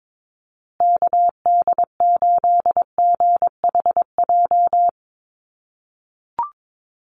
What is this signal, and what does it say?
Morse code, heard as a keyed tone, about 22 wpm, 700 Hz. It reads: KD8GHJ
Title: USA Call Signs Repeated 2x 22wpm